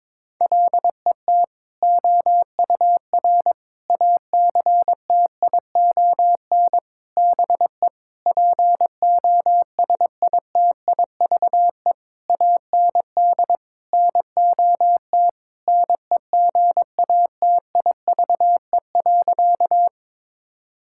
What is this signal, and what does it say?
Morse code, heard as a keyed tone, about 22 wpm, 700 Hz. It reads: LET OUR ACTION BE POSITIVE AND NOT NEGATIVE.
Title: Sentences from Top 1000 Words - Review Part 6 - 22wpm